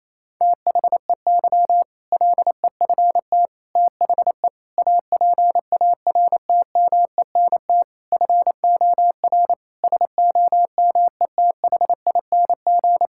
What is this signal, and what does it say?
Morse code, heard as a keyed tone, about 28 wpm, 700 Hz. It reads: THEY LEFT THE APARTMENT FOR SOMETHING
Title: Sentences from Top 1000 Words - Review Part 6 - 28wpm